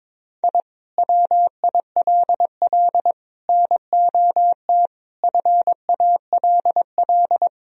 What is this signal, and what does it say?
Morse code, heard as a keyed tone, about 22 wpm, 700 Hz. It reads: I WILL NOT FALL
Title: Sentences from Top 1000 Words - Review Part 2 - 22wpm